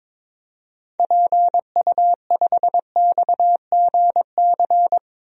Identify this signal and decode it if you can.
Morse code, heard as a keyed tone, about 22 wpm, 700 Hz. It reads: PU5XGC
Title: DXCC-100 Call Signs - 6 Characters - 22wpm